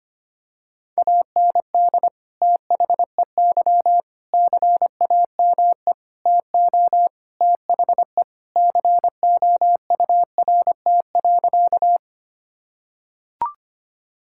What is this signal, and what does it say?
Morse code, heard as a keyed tone, about 25 wpm, 700 Hz. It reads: AND THEY CAME TO THE COURT.
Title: Sentences from Top 1600 Words 25wpm